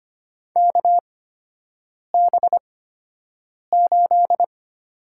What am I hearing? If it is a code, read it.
Morse code, heard as a keyed tone, about 25 wpm, spaced as for 7 wpm, 700 Hz. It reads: KB8
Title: USA Call Signs - 6 Characters - 8x Character Spacing - 25wpm